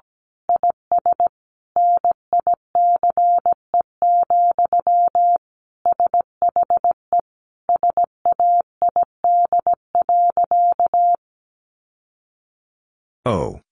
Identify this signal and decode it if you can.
Morse code, heard as a keyed tone, about 17 wpm, 700 Hz. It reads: IS NICE, SHE SAID.
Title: Sentences from Top 1100 Words 17wpm